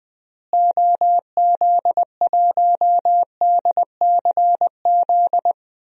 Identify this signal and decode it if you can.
Morse code, heard as a keyed tone, about 20 wpm, 700 Hz. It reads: OZ1DCZ
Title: DXCC-100 Call Signs - 6 Characters - Repeated 2x - 20wpm